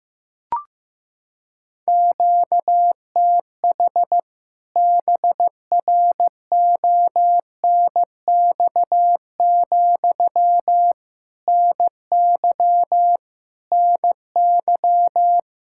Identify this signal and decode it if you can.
Morse code, heard as a keyed tone, about 15 wpm, 700 Hz. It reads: QTH BRONX, NY NY
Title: K1USN SST QSO Practice 15wpm